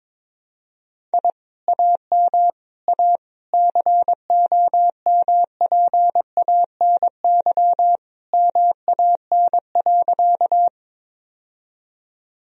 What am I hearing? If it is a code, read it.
Morse code, heard as a keyed tone, about 22 wpm, 700 Hz. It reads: I AM A COMPANY MAN.
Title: Sentences from Top 1000 Words - Review Part 3 - 22wpm